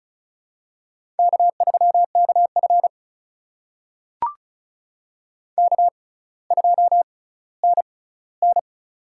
Morse code, heard as a keyed tone, about 36 wpm, spaced as for 18 wpm, 700 Hz. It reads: K3KF K2NN
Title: USA Call Signs - 4 Characters - 6x Character Spacing - 35wpm